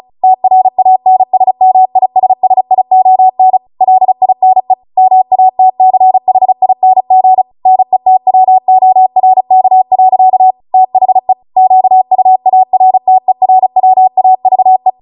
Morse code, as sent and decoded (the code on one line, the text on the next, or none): - .-. .- -. ... -- .. ... ... .. --- -. / .-.. .. -. / -- .- - -.-. .... .. -. --. / -. - .-- --- .-. -.- .-.-.- / - .... / --.- ..- .- .-. - . .-. .-- .- ...- .
TRANSMISSION LINE MATCHING NETWORK. THE QUARTERWAVE